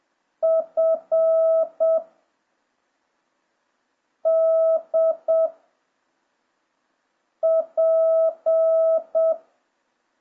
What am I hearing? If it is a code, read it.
Morse code, heard as a keyed tone, about 7 wpm, 650 Hz. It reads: F D P